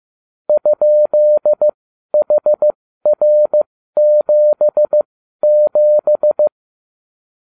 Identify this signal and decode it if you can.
Morse code, heard as a keyed tone, about 15 wpm, spaced as for 12 wpm, 600 Hz. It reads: ?HR77